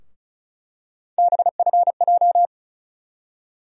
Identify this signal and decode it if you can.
Morse code, heard as a keyed tone, about 36 wpm, 700 Hz. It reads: BFJ
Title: USA Call Sign Postfixes - 3 Letters 35wpm